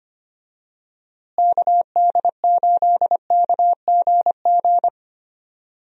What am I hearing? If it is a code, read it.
Morse code, heard as a keyed tone, about 25 wpm, 700 Hz. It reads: KD8KGG
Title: USA Call Signs - 6 Characters - Repeated 3x - 25wpm